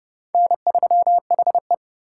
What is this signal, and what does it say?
Morse code, heard as a keyed tone, about 36 wpm, 700 Hz. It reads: N3HE
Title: Speed Racing - USA 1x2 Call Signs 30wpm